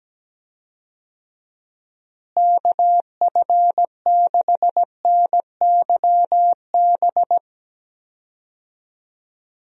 Morse code, heard as a keyed tone, about 17 wpm, 700 Hz. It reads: KF6NYB
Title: USA Call Signs - 6 Characters - Repeated 3x - 17wpm